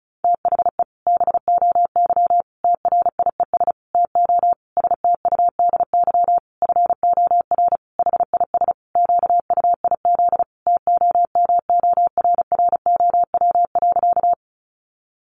Morse code, heard as a keyed tone, about 34 wpm, 700 Hz. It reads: THE BOY TRIES TO STUDY FOR HIS QUIZ TOMORROW.
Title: Additional Easy Phrases 35wpm